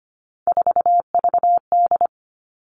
Morse code, heard as a keyed tone, about 25 wpm, 700 Hz. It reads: 4VD
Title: All Cut Numbers - 3-Digit 25wpm